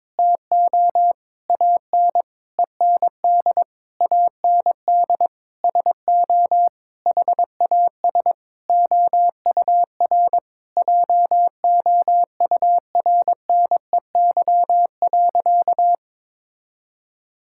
Morse code, heard as a keyed tone, about 22 wpm, 700 Hz. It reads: TO AN END AND SO HAS OUR JOURNEY.
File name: Sentences with 9-Word Phrases - 22wpm